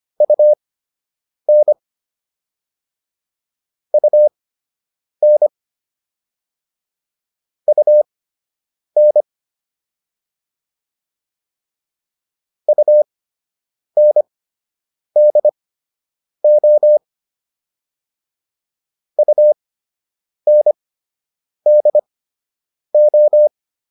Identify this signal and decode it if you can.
Morse code, heard as a keyed tone, about 25 wpm, spaced as for 8 wpm, 600 Hz. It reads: UN UN UN UNDO UNDO